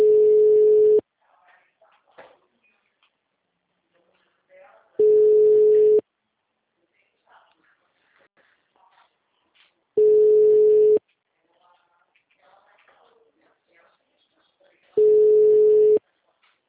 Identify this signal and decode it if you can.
Ringback tone (Europe, 425 Hz), with speech.